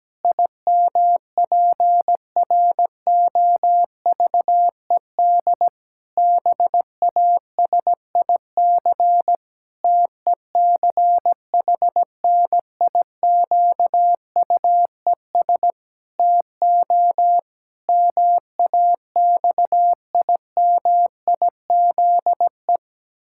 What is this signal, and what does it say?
Morse code, heard as a keyed tone, about 17 wpm, 700 Hz. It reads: IMPROVED BASIC TECHNIQUES TO MAXIMIZE
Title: Pangram Challenge v2 - 17wpm